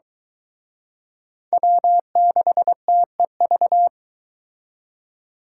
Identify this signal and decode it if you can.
Morse code, heard as a keyed tone, about 23 wpm, 700 Hz. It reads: W6TEV